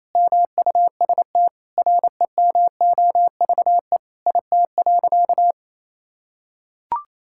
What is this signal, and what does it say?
Morse code, heard as a keyed tone, about 28 wpm, 700 Hz. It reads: MUST REMOVE IT.
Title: Sentences from Top 1000 Words - Review Part 5 - 28wpm